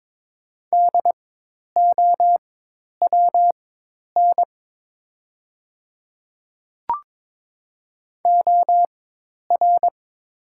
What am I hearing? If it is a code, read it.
Morse code, heard as a keyed tone, about 22 wpm, spaced as for 10 wpm, 700 Hz. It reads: DOWN OR
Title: 4x Character Spacing - Top 100 Words - 22wpm